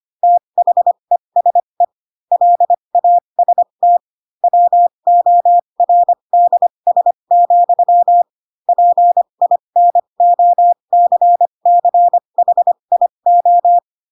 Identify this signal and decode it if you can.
Morse code, heard as a keyed tone, about 25 wpm, spaced as for 22 wpm, 700 Hz. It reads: THESE LAST WORDS, PINOCCHIO
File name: Pinocchio / Pinocchio ch4 18 25 ik1hge